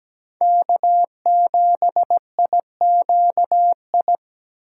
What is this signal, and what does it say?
Morse code, heard as a keyed tone, about 20 wpm, 700 Hz. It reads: K7IQI
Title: Speed Racing - USA Call Signs 17wpm